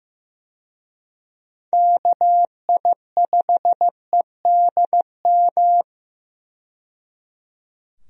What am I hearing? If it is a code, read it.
Morse code, heard as a keyed tone, about 16 wpm, 700 Hz. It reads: KI5EDM
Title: USA Call Signs - 6 Characters - 15wpm